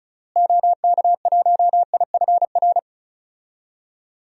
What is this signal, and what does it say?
Morse code, heard as a keyed tone, about 36 wpm, 700 Hz. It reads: OK1IFR
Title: DXCC-100 Call Signs - 6 Characters - Repeated 3x - 35wpm